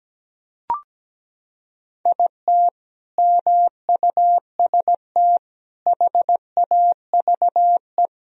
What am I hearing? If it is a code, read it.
Morse code, heard as a keyed tone, about 17 wpm, 700 Hz. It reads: IT MUST HAVE
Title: Sentences from Top 1000 Words - Review Part 1 - 17wpm